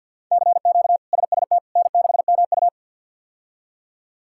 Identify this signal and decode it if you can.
Morse code, heard as a keyed tone, about 52 wpm, 700 Hz. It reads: CQ SST N6MU
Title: K1USN SST QSO Practice - 50wpm